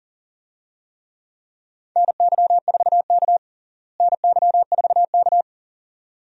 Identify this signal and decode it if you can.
Morse code, heard as a keyed tone, about 40 wpm, 700 Hz. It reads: NY4K NY4K
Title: USA Call Signs - 4 Characters - Repeated 2x - 40wpm